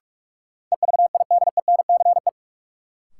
Morse code, heard as a keyed tone, about 46 wpm, 700 Hz. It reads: EVIDENCE
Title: Sentences from Top 1300 Words - New Word Drill - 45wpm